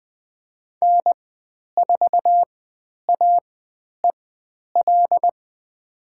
Morse code, heard as a keyed tone, about 20 wpm, spaced as for 10 wpm, 700 Hz. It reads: N4AEL
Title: 102 Lesson 17g - Letter u - Call Signs 20wpm